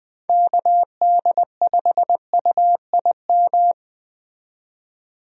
Morse code, heard as a keyed tone, about 20 wpm, 700 Hz. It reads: KD5UIM
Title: USA Call Signs - 6 Characters - 20wpm